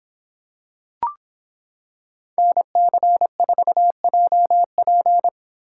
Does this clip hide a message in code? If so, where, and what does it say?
Yes: Morse code from 2.4 to 5.3 s, heard as a keyed tone, about 25 wpm, 700 Hz. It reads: NC4JP